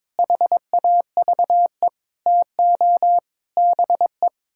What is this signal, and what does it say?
Morse code, heard as a keyed tone, about 21 wpm, 700 Hz. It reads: HAVE TO BE
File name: practice / 22wpm / Build A Phrase - 5-Word Phrases - 22wpm